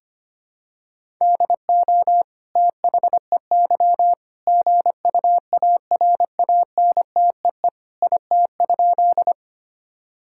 Morse code, heard as a keyed tone, about 25 wpm, 700 Hz. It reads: DO THEY GUARANTEE IT?
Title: Sentences from Top 1600 Words 25wpm